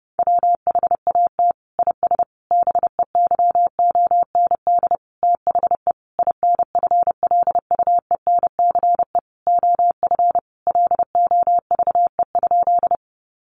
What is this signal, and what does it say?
Morse code, heard as a keyed tone, about 30 wpm, 700 Hz. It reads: WHAT IS BEYOND THE INFLUENCE OF LOVE?
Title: Sentences from Top 1100 Words 30wpm